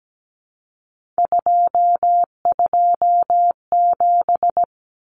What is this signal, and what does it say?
Morse code, heard as a keyed tone, about 17 wpm, 700 Hz. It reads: 227
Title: Common Cut Numbers - 3-Digit 17wpm